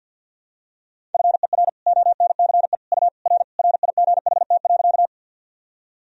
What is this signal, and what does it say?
Morse code, heard as a keyed tone, about 51 wpm, 700 Hz. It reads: FER ONCE U R RIGHT.
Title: Sentences from Top 300 Words and Abbreviations 50wpm